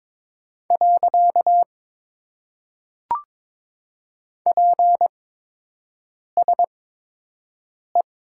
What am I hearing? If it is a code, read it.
Morse code, heard as a keyed tone, about 22 wpm, 700 Hz. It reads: . P S E
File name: Top 90 QSO Elements - 8x Character Spacing - 22wpm